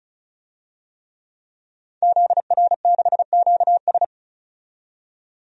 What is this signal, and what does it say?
Morse code, heard as a keyed tone, about 34 wpm, 700 Hz. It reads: ZR6QS
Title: DXCC-100 Call Signs - 5 Characters - 35wpm